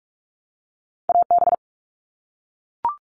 Morse code, heard as a keyed tone, about 44 wpm, 700 Hz. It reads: AB
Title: Canadian Provinces and Territories - Repeated 3x - 45wpm